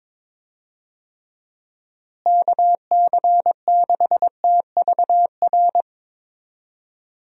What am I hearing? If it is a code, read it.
Morse code, heard as a keyed tone, about 21 wpm, 700 Hz. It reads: KC6TVR
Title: USA Call Signs - 6 Characters - Repeated 3x - 22wpm